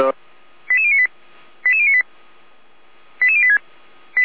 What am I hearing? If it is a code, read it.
Five-tone selective call (ZVEI), heard as quick runs of five tones, tones of about 70 ms, with speech. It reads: 88098 88098 88086